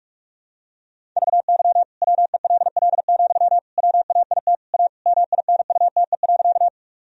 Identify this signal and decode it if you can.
Morse code, heard as a keyed tone, about 41 wpm, 700 Hz. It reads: VY WELL, WAIT A MINUTE.
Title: Sentences from Top 500 Words and Abbreviations 45wpm